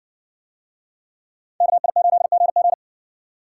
Morse code, heard as a keyed tone, about 61 wpm, 700 Hz. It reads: KI8GG